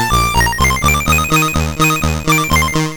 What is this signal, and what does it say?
There is no code.